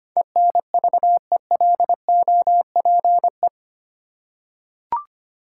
Morse code, heard as a keyed tone, about 25 wpm, 700 Hz. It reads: ENVELOPE
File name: Eight Letter Words - 25wpm